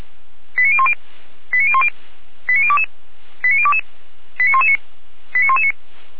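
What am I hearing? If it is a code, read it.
Five-tone selective call (ZVEI), heard as quick runs of five tones, tones of about 70 ms. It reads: 89010 89010 89022 89022 89119 89119
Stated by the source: Received on an SDR.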